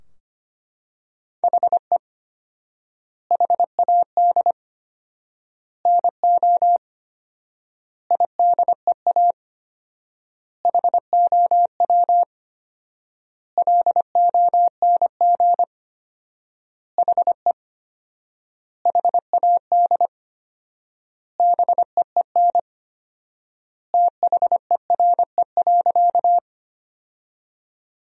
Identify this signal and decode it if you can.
Morse code, heard as a keyed tone, about 25 wpm, 700 Hz. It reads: HE HAD NO IDEA HOW LONG HE HAD BEEN THERE.
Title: 4x Word-Spacing - Sentences from Top 300 Words - 25wpm